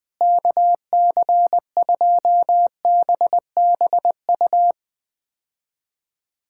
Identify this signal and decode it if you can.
Morse code, heard as a keyed tone, about 20 wpm, 700 Hz. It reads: KC2BBU
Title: K1USN SST QSO Practice - 20wpm